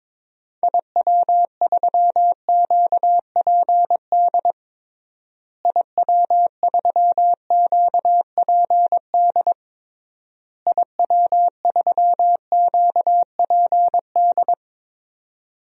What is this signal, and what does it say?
Morse code, heard as a keyed tone, about 22 wpm, 700 Hz. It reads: IW3QPD IW3QPD IW3QPD
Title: DXCC-100 Call Signs - 6 Characters - Repeated 3x - 22wpm